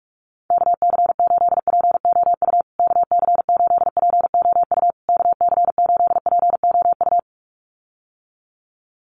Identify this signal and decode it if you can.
Morse code, heard as a keyed tone, about 44 wpm, 700 Hz. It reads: KC8POU KC8POU KC8POU
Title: USA Call Signs Repeated 3x 45wpm